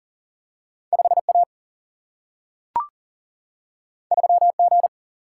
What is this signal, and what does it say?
Morse code, heard as a keyed tone, about 39 wpm, 700 Hz. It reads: 5 A 3 G
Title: Single-Letter Single-Number 40wpm